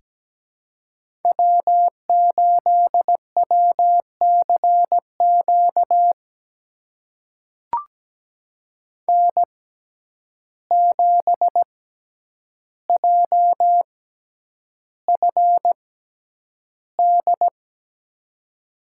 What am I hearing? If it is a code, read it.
Morse code, heard as a keyed tone, about 17 wpm, spaced as for 9 wpm, 700 Hz. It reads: W8WCQ N7JFD
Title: USA Call Signs - 5 Characters - 6x Character Spacing - 17wpm